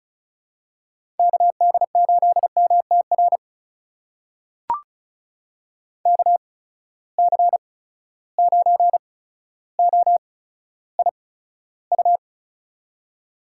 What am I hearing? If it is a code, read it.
Morse code, heard as a keyed tone, about 36 wpm, spaced as for 15 wpm, 700 Hz. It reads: KD8MTR KC9OIU
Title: USA Call Signs - 6 Characters - 8x Character Spacing - 35wpm